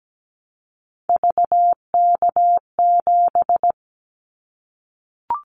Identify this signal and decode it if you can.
Morse code, heard as a keyed tone, about 17 wpm, 700 Hz. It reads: VK7
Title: Callsign Prefixes for Top 100 DXCC - 17wpm